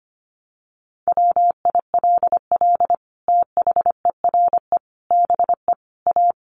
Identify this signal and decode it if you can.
Morse code, heard as a keyed tone, about 25 wpm, 700 Hz. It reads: WILL THERE BE A
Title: Sentences from Top 1300 Words 25wpm